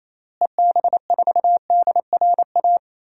Morse code, heard as a keyed tone, about 28 wpm, 700 Hz. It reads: EB4DRA
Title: DXCC-100 Call Signs - 6 Characters - Repeated 3x - 28wpm